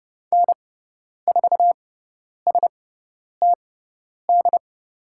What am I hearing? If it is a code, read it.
Morse code, heard as a keyed tone, about 30 wpm, spaced as for 10 wpm, 690 Hz. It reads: N4STD
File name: Lesson 17g - Letter u - Call Signs 30wpm